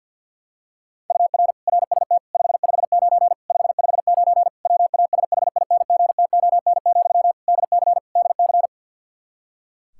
Morse code, heard as a keyed tone, about 50 wpm, 700 Hz. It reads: UR RST 559 559 WASHINGTON, DC DC
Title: Practice QSO - Exchange 1 - Beginning - 50wpm